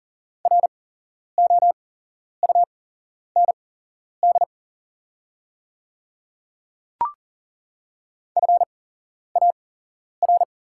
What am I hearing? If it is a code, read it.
Morse code, heard as a keyed tone, about 37 wpm, spaced as for 11 wpm, 700 Hz. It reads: ROUND FAR
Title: 8x Character Spacing - Top 500 Words - 40wpm